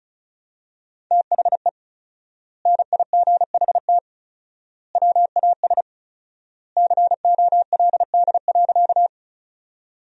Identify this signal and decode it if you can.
Morse code, heard as a keyed tone, about 36 wpm, 700 Hz. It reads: THE NIGHT WAS COLD.
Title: 4x Word-Spacing - Sentences from Top 400 Words - 35wpm